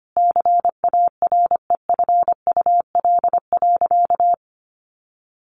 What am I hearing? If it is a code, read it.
Morse code, heard as a keyed tone, about 25 wpm, 700 Hz. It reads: CAREFUL.
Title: A Sound Of Thunder 25wpm